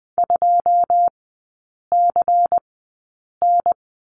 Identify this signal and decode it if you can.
Morse code, heard as a keyed tone, about 20 wpm, spaced as for 8 wpm, 690 Hz. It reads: 2CN
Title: 159 Lesson 26e - Letter g - Three Characters 20wpm